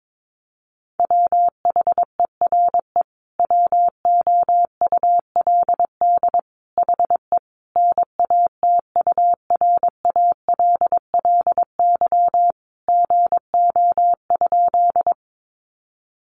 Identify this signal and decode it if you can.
Morse code, heard as a keyed tone, about 22 wpm, 700 Hz. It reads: WHERE WOULD HE NATURALLY GO?
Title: Sentences from Top 1300 Words 22wpm